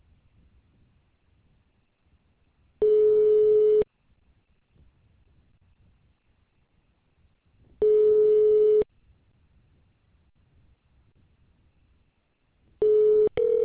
Ringback tone (Europe, 425 Hz).